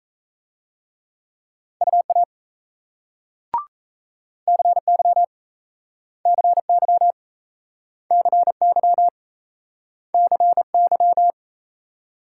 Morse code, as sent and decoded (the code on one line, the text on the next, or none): ..- .- / -.-. -.-- / -.-. -.-- / -.-. -.-- / -.-. -.--
UA CY CY CY CY